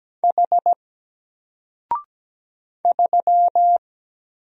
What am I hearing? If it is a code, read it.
Morse code, heard as a keyed tone, about 17 wpm, 700 Hz. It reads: H 3
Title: Single Character (Letter:Number:Punctuation:Prosign) 17wpm